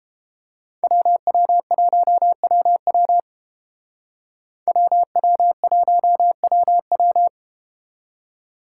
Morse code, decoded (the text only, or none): WW1WW WW1WW